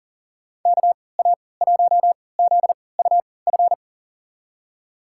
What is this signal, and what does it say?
Morse code, heard as a keyed tone, about 40 wpm, spaced as for 23 wpm, 700 Hz. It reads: KA1ZUF
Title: USA Call Signs - 6 Characters - 3x Character Spacing - 40wpm